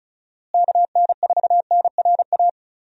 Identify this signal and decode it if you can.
Morse code, heard as a keyed tone, about 34 wpm, 700 Hz. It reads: KN4NRA
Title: USA Call Signs - 6 Characters - Repeated 3x - 35wpm